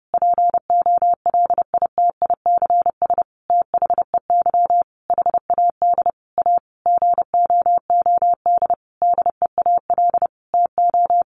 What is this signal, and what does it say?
Morse code, heard as a keyed tone, about 30 wpm, 700 Hz. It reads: POLITICS THEY HAD A GOOD DEAL TO
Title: Sentences from Top 1000 Words 30wpm